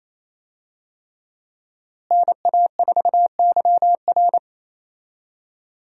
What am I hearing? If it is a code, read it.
Morse code, heard as a keyed tone, about 28 wpm, 700 Hz. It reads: NA4YR
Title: USA Call Signs - 5 Characters - Repeated 3x - 28wpm